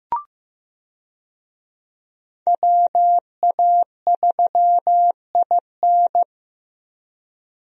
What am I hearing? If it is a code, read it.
Morse code, heard as a keyed tone, about 15 wpm, 700 Hz. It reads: WA3IN
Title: USA Call Signs - 5 Characters - Repeated 2x - 15wpm